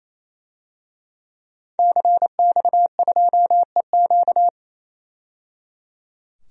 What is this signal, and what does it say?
Morse code, heard as a keyed tone, about 28 wpm, 700 Hz. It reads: CX2EQ
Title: DXCC-100 Call Signs - 5 Characters - 28wpm